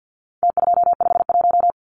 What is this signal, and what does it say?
Morse code, heard as a keyed tone, about 50 wpm, 700 Hz. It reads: T251
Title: Numbers - Contest Style - 4-Digit 50wpm